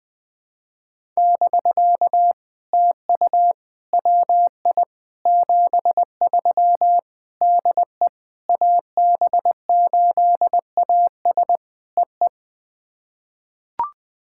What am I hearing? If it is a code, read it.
Morse code, heard as a keyed tone, about 20 wpm, 700 Hz. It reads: <BK> TU WI 73 DE AB8AS EE
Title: POTA QSO Practice - 20wpm